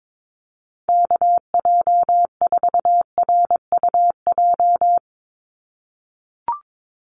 Morse code, heard as a keyed tone, about 22 wpm, 700 Hz. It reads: KJ4RUJ
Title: USA Call Signs 22wpm